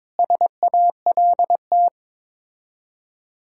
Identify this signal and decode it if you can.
Morse code, heard as a keyed tone, about 22 wpm, 700 Hz. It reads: SALT
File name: practice / 22wpm / Build A Phrase - 5-Word Phrases - 22wpm